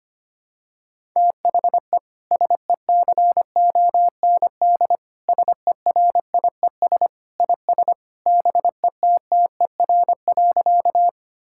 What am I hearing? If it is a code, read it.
Morse code, heard as a keyed tone, about 25 wpm, 700 Hz. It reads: THE SECOND SERIES IS BETTER.
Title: Sentences from Top 1000 Words - Review Part 5 - 25wpm